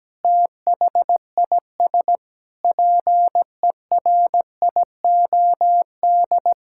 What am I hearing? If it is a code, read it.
Morse code, heard as a keyed tone, about 17 wpm, 700 Hz. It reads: THIS PERIOD
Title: Sentences from Top 1000 Words - Review Part 5 - 17wpm